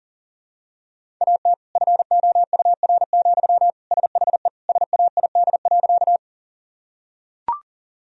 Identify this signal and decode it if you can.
Morse code, heard as a keyed tone, about 40 wpm, 700 Hz. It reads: AT FOUR, SHE SAID.
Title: Sentences from Top 1000 Words - Review Part 1 - 40wpm